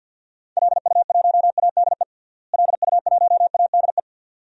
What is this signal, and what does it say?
Morse code, heard as a keyed tone, about 50 wpm, 700 Hz. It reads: LU1ADE LU1ADE
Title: DXCC-100 Call Signs - 6 Characters - Repeated 2x - 50wpm